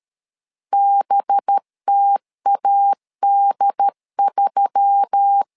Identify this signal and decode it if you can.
Morse code, heard as a keyed tone, about 13 wpm, 800 Hz. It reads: BTAD3
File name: LEZIONI / 5 LBVC / 5 LB 12wpm 1906